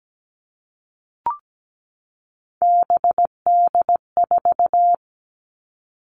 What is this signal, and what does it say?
Morse code, heard as a keyed tone, about 17 wpm, 700 Hz. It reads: BD4
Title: Callsign Prefixes for Top 100 DXCC - 17wpm